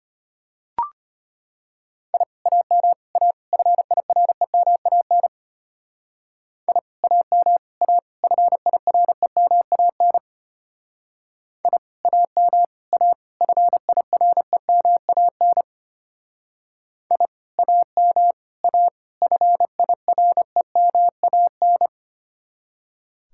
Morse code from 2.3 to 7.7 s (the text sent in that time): AM A FIREMAN I AM